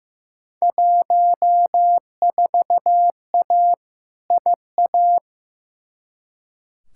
Morse code, heard as a keyed tone, about 15 wpm, 700 Hz. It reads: 14A IA
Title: Field Day Exchanges Encore - 15wpm